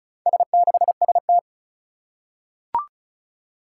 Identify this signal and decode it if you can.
Morse code, heard as a keyed tone, about 36 wpm, 700 Hz. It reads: S6ST
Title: Four Challenging Letter-Number Combinations 35wpm